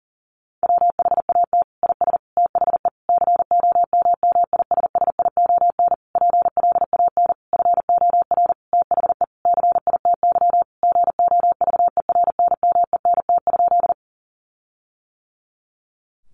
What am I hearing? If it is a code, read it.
Morse code, heard as a keyed tone, about 40 wpm, 700 Hz. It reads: WHAT IS THE COMMISSION PLAN FOR THE CITY GOVERNMENT?
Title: Sentences from Top 1400 Words 40wpm